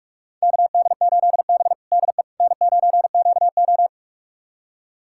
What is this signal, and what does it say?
Morse code, heard as a keyed tone, about 46 wpm, 700 Hz. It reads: KD8B DE N9QO
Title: USA Callsign DE Callsign - 4 Characters - 45wpm